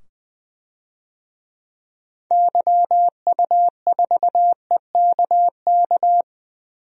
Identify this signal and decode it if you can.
Morse code, heard as a keyed tone, about 20 wpm, 700 Hz. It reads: YU4EKK
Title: DXCC-100 Call Signs - 6 Characters - 20wpm